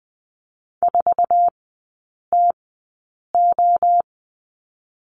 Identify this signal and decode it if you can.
Morse code, heard as a keyed tone, about 20 wpm, spaced as for 8 wpm, 690 Hz. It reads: 4TO
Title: 081 Lesson 14e - Number 2 - Three Characters 20wpm